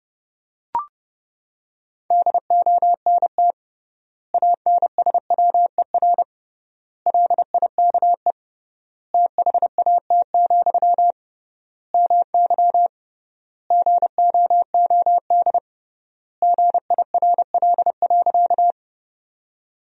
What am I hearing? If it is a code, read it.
Morse code, heard as a keyed tone, about 28 wpm, 700 Hz. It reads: DONT ANSWER LIKE THAT, MY GOOD GIRL.
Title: 3x Word-Spacing - Sentences from Top 400 Words - 30wpm